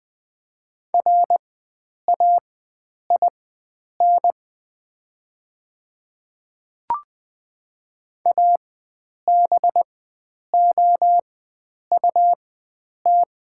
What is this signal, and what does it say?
Morse code, heard as a keyed tone, about 21 wpm, spaced as for 9 wpm, 690 Hz. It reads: RAIN ABOUT
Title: 4x Character Spacing - Top 300 Words - 20wpm